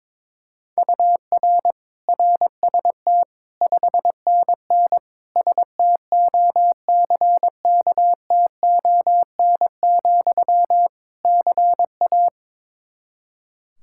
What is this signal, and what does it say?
Morse code, heard as a keyed tone, about 22 wpm, 700 Hz. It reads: UR RST 5NN STOCKTON, CA
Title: K1USN SST QSO Practice 22wpm